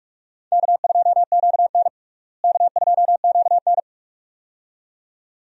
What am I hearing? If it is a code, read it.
Morse code, heard as a keyed tone, about 46 wpm, 700 Hz. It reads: K2QN K2QN
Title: USA Call Signs - 4 Characters - Repeated 2x - 45wpm